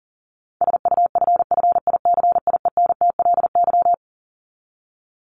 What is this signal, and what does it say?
Morse code, heard as a keyed tone, about 40 wpm, 700 Hz. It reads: SUFFICIENTLY
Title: LY Suffix - Top 100 Words with Base Words 40wpm